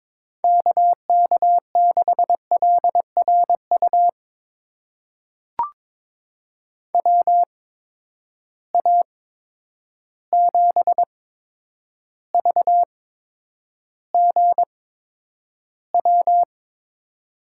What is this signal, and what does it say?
Morse code, heard as a keyed tone, about 22 wpm, spaced as for 9 wpm, 700 Hz. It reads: KK6LRU WA7VGW
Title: USA Call Signs - 6 Characters - 8x Character Spacing - 22wpm